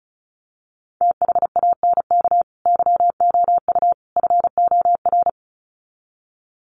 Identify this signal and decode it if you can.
Morse code, heard as a keyed tone, about 34 wpm, 700 Hz. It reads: THANK YOU FOR
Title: Sets of 3 Words 35wpm